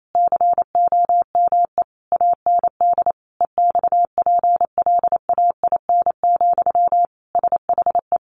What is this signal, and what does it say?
Morse code, heard as a keyed tone, about 28 wpm, 700 Hz. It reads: COME AND EXPLAIN, SHE
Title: Sentences from Top 1100 Words 28wpm